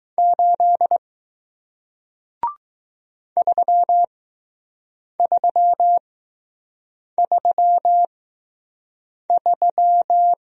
Morse code, heard as a keyed tone, about 20 wpm, spaced as for 6 wpm, 700 Hz. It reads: 8 3333